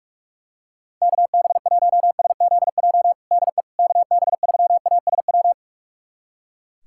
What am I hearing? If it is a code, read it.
Morse code, heard as a keyed tone, about 41 wpm, 700 Hz. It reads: KB1SZJ DE KB3ASW
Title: USA Callsign DE Callsign - 6 Characters - 45wpm